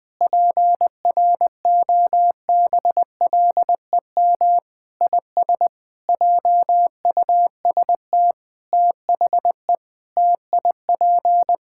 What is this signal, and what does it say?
Morse code, heard as a keyed tone, about 20 wpm, 700 Hz. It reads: PROBLEM IS JUST THE TIP
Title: Sentences with 7-Word Phrases - 20wpm